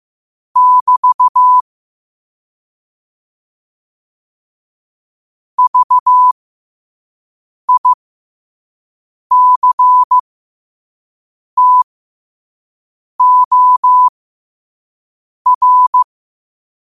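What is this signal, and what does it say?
Morse code, heard as a keyed tone, about 15 wpm, spaced as for 5 wpm, 1000 Hz. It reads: <BT> VICTOR
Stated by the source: Morse at 5 wpm.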